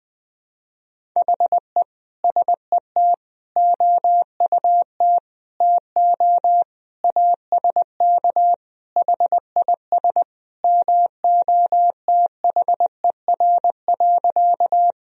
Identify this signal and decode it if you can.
Morse code, heard as a keyed tone, about 20 wpm, 700 Hz. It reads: HE SET OUT TO ASK HIS MOTHER.
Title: Sentences from Top 1000 Words - Review Part 1 - 20wpm